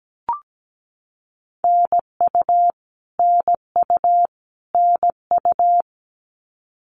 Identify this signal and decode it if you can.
Morse code, heard as a keyed tone, about 17 wpm, 700 Hz. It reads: NU NU NU
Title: Canadian Provinces and Territories - Repeated 3x - 17wpm